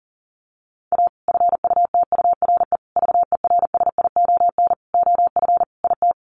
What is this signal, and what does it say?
Morse code, heard as a keyed tone, about 40 wpm, 700 Hz. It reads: A FUTURE VERSION OF IT
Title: Sentences from Top 1400 Words 40wpm